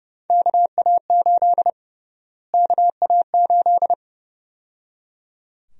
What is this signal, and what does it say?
Morse code, heard as a keyed tone, about 30 wpm, 700 Hz. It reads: KA8 KA8
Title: USA Call Sign Prefixes - Repeated 2x - 30wpm